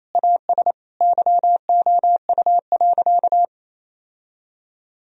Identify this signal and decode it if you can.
Morse code, heard as a keyed tone, about 28 wpm, 700 Hz. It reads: AS YOU.
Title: Sentences from Top 1000 Words - Review Part 3 - 28wpm